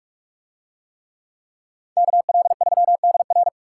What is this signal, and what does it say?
Morse code, heard as a keyed tone, about 41 wpm, 700 Hz. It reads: KL3DR